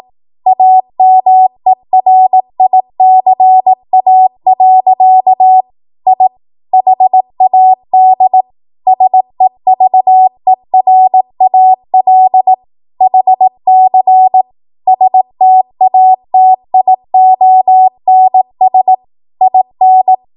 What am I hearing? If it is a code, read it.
Morse code, heard as a keyed tone, about 18 wpm, 750 Hz. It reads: AMERICA. I HAD SEVERAL HC STATIONS IN
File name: Archive / 18 WPM / 18WPM